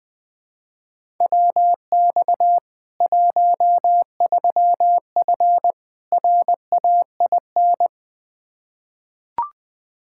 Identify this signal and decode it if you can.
Morse code, heard as a keyed tone, about 19 wpm, 700 Hz. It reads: WX 13F RAIN
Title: K1USN SST QSO Practice 20wpm